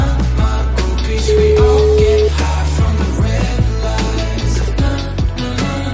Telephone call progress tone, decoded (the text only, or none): ringback tone